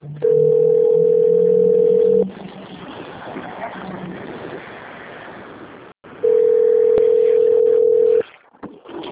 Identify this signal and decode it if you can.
Ringback tone (North America), 440 + 480 Hz.
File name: rg-1600-unknown- - - .23516